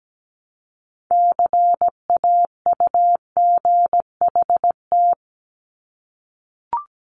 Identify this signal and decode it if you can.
Morse code, heard as a keyed tone, about 17 wpm, 700 Hz. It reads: CAUGHT
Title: Multi-letter Phonemes 17wpm